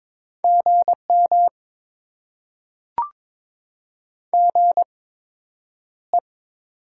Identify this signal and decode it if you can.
Morse code, heard as a keyed tone, about 22 wpm, spaced as for 9 wpm, 700 Hz. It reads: GM GE
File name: Top 20 QSO Elements - 8x Character Spacing - 22wpm